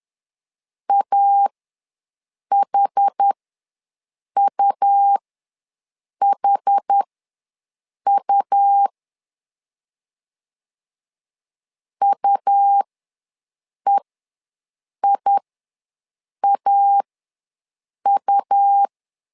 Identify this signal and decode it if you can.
Morse code, heard as a keyed tone, about 10 wpm, spaced as for 6 wpm, 800 Hz. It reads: AHUHU UEIAU
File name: LEZIONI / 2 UHDN / 2 UH ws 10wpm 2029